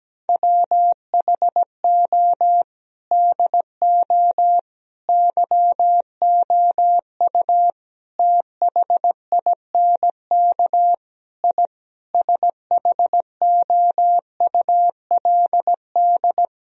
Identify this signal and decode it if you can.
Morse code, heard as a keyed tone, about 17 wpm, 700 Hz. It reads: WHO DO YOU THINK I SHOULD
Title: Sentences from Top 1000 Words - Review Part 4 - 17wpm